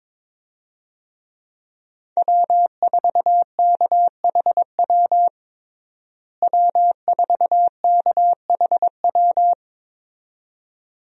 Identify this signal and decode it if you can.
Morse code, heard as a keyed tone, about 21 wpm, 700 Hz. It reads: W4KHW W4KHW
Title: USA Call Signs - 5 Characters - Repeated 2x - 22wpm